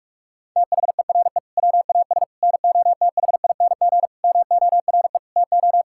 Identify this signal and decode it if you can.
Morse code, heard as a keyed tone, about 41 wpm, 700 Hz. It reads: THERE WAS NOTHING MORE TO
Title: Sentences from Top 1000 Words - Review Part 5 - 45wpm